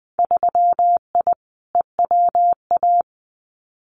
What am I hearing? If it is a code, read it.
Morse code, heard as a keyed tone, about 20 wpm, 700 Hz. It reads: 3I EWA
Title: Winter Field Day Exchanges Encore 20wpm